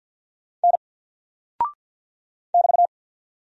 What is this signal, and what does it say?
Morse code, heard as a keyed tone, about 48 wpm, 700 Hz. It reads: N <BT>
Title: Single Character (Letter:Number:Punctuation:Prosign) Alt 50wpm